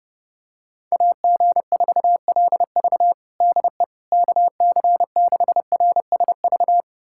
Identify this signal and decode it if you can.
Morse code, heard as a keyed tone, about 30 wpm, 700 Hz. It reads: AG4LV DE KC6RSV
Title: USA Callsign DE Callsign 30wpm